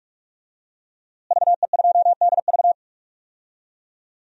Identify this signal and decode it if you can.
Morse code, heard as a keyed tone, about 43 wpm, 700 Hz. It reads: VE2DV
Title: DXCC-100 Call Signs - 5 Characters - Repeated 2x - 45wpm